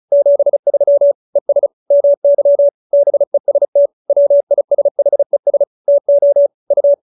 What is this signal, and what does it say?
Morse code, heard as a keyed tone, about 34 wpm, 570 Hz. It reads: 73 ES MY BEST WISHES TO U